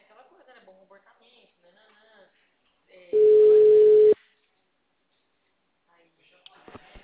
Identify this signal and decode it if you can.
Ringback tone (Europe, 425 Hz), with speech.
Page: Marilia - SP